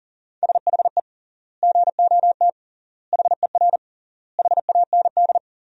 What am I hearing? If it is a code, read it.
Morse code, heard as a keyed tone, about 40 wpm, 700 Hz. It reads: SHE GOT HER HAND
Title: 3x Word-Spacing - Sentences from Top 500 Words - 40wpm